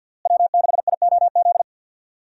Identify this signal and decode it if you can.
Morse code, heard as a keyed tone, about 53 wpm, 700 Hz. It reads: W6IOZ
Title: USA Call Signs - 5 Characters - Repeated 2x - 50wpm